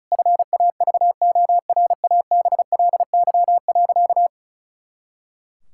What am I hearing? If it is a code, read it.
Morse code, heard as a keyed tone, about 34 wpm, 700 Hz. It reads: FAVORABLY.
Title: Pangram Challenge v2 - 35wpm